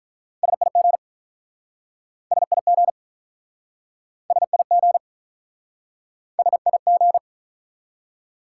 Morse code, heard as a keyed tone, about 47 wpm, 700 Hz. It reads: SIG SIG SIG SIG